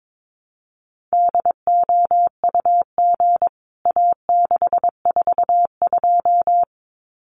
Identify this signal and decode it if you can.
Morse code, heard as a keyed tone, about 22 wpm, 700 Hz. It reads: DOUG A642
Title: CWT Exchange 22wpm